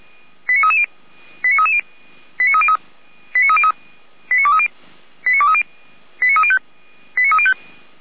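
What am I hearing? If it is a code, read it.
Five-tone selective call (ZVEI), heard as quick runs of five tones, tones of about 70 ms. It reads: 89330 89330 89303 89303 89230 89230 89305 89305
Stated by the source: ZVEI five-tone call 89319 received by SDR, from the server of Feuerwehr Orlamünde (fire brigade).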